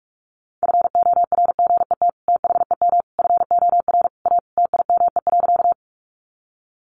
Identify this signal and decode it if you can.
Morse code, heard as a keyed tone, about 44 wpm, 700 Hz. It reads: FORGET THEM FOR A TIME.
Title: Sentences from Top 1200 Words 45wpm